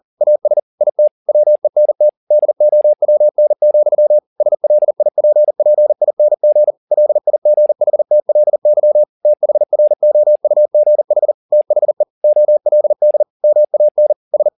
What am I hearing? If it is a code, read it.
Morse code, heard as a keyed tone, about 40 wpm, 600 Hz. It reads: AS IT WENT DOWN, SLIPPING LIGHTLY THROUGH THE OLD MAN S